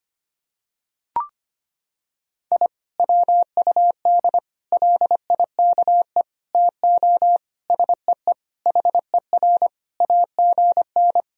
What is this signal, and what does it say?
Morse code, heard as a keyed tone, about 25 wpm, 700 Hz. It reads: I WUD LIKE TO SEE HER AGN
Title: Sentences from Top 200 Words and Abbreviations 25wpm